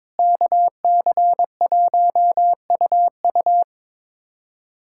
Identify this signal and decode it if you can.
Morse code, heard as a keyed tone, about 22 wpm, 700 Hz. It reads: KC1UU
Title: USA Callsign DE Callsign - 5 Characters - 22wpm